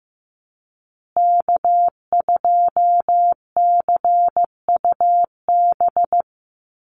Morse code, heard as a keyed tone, about 14 wpm, 700 Hz. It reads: K2CUB
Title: Call Signs Repeated 3x - 15wpm